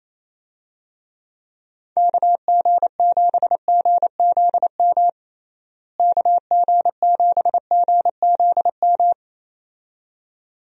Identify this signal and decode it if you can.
Morse code, heard as a keyed tone, about 28 wpm, 700 Hz. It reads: KG7GZM KG7GZM
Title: USA Call Signs - 6 Characters - Repeated 2x - 28wpm